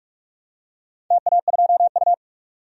Morse code, heard as a keyed tone, about 46 wpm, 700 Hz. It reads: TA2U
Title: DXCC-100 Call Signs - 4 Characters - 45wpm